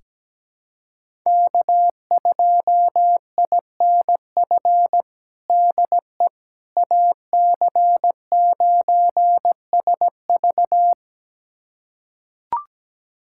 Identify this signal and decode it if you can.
Morse code, heard as a keyed tone, about 17 wpm, 700 Hz. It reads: K2INF DE AC9SV
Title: USA Callsign DE Callsign - 5 Characters - 17wpm